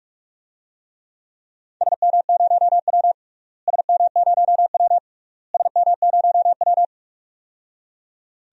Morse code, heard as a keyed tone, about 46 wpm, 700 Hz. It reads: SM0W SM0W SM0W
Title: DXCC-100 Call Signs - 4 Characters - Repeated 3x - 45wpm